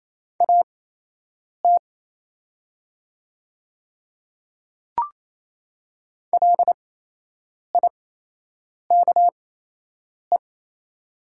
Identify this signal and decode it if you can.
Morse code, heard as a keyed tone, about 28 wpm, spaced as for 8 wpm, 700 Hz. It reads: AT LIKE
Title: 8x Character Spacing - Top 100 Words - 28wpm